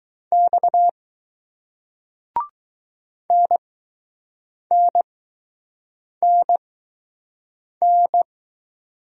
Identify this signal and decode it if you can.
Morse code, heard as a keyed tone, about 20 wpm, spaced as for 6 wpm, 700 Hz. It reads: X NNNN